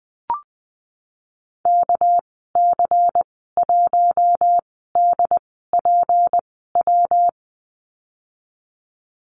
Morse code, heard as a keyed tone, about 20 wpm, spaced as for 14 wpm, 700 Hz. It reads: KC1DPW
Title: 200 Lesson 32g - Letter k - Call Signs 20wpm